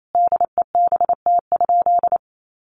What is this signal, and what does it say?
Morse code, heard as a keyed tone, about 28 wpm, 700 Hz. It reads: DEBT?
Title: Sentences from Top 1300 Words 28wpm